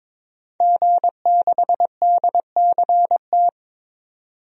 Morse code, heard as a keyed tone, about 22 wpm, 700 Hz. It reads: G6DCT
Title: DXCC-100 Call Signs - 5 Characters - 22wpm